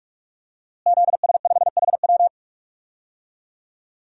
Morse code, heard as a keyed tone, about 43 wpm, 700 Hz. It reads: ZS5HW
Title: DXCC-100 Call Signs - 5 Characters - Repeated 3x - 45wpm